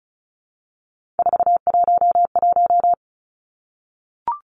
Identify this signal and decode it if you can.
Morse code, heard as a keyed tone, about 34 wpm, 700 Hz. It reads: 411
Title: Numbers - 3-Digit 35wpm